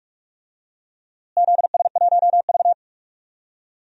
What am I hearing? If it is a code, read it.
Morse code, heard as a keyed tone, about 46 wpm, 700 Hz. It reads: ZS1V
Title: DXCC-100 Call Signs - 4 Characters - Repeated 3x - 45wpm